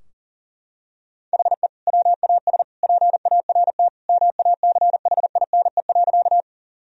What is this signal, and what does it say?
Morse code, heard as a keyed tone, about 40 wpm, 700 Hz. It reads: HE WAS PART MACHINE.
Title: Sentences from Top 1000 Words - Review Part 2 - 40wpm